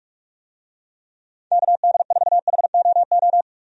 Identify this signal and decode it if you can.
Morse code, heard as a keyed tone, about 43 wpm, 700 Hz. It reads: KD4HOO
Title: USA Call Signs - 6 Characters - 45wpm